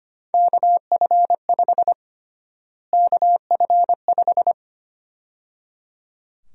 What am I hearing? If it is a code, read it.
Morse code, heard as a keyed tone, about 25 wpm, 700 Hz. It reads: KF5 KF5
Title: USA Call Sign Prefixes - Repeated 2x - 25wpm